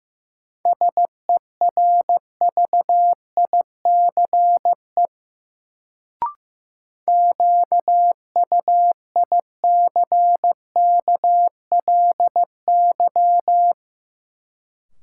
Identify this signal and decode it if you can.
Morse code, heard as a keyed tone, about 15 wpm, 700 Hz. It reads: SERVICE QUICKLY
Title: Sentences from Top 700 Words - New Word Drill - 15wpm